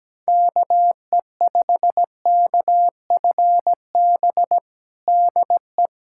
Morse code, heard as a keyed tone, about 17 wpm, 700 Hz. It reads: KE5KFB DE
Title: USA Callsign DE Callsign 17wpm